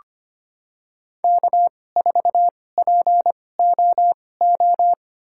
Morse code, heard as a keyed tone, about 25 wpm, spaced as for 18 wpm, 700 Hz. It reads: K4POO
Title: USA Call Signs - 5 Characters - 2x Character Spacing - 25wpm